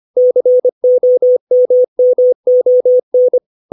Morse code, heard as a keyed tone, about 25 wpm, 500 Hz. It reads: COMMON
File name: 632 words / 632 Words 25WPM